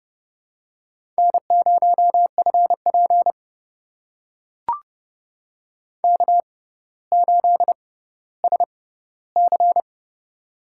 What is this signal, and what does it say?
Morse code, heard as a keyed tone, about 30 wpm, spaced as for 15 wpm, 700 Hz. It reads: N0FP K8SC
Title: USA Call Signs - 4 Characters - 6x Character Spacing - 30wpm